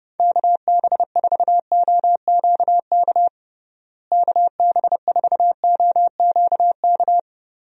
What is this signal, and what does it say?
Morse code, heard as a keyed tone, about 32 wpm, 700 Hz. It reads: KB4OQK KB4OQK
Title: USA Call Signs - 6 Characters - Repeated 3x - 30wpm